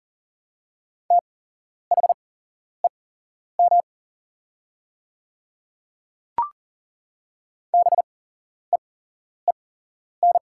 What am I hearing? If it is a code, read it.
Morse code, heard as a keyed tone, about 40 wpm, spaced as for 11 wpm, 700 Hz. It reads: THEM BEEN
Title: 8x Character Spacing - Top 100 Words - 40wpm